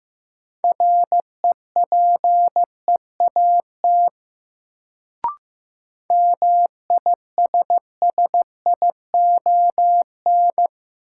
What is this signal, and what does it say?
Morse code, heard as a keyed tone, about 15 wpm, 700 Hz. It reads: REPEAT MISSION
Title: Sentences from Top 1600 Words - New Word Drill Alt - 15wpm